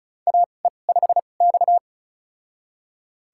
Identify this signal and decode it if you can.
Morse code, heard as a keyed tone, about 35 wpm, spaced as for 25 wpm, 700 Hz. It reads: AE5X
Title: USA Call Signs - 4 Characters - 2x Character Spacing - 35wpm